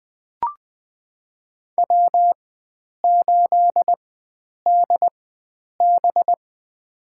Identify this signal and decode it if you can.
Morse code, heard as a keyed tone, about 20 wpm, spaced as for 9 wpm, 700 Hz. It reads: W8DB
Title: USA Call Signs - 4 Characters - 4x Character Spacing - 20wpm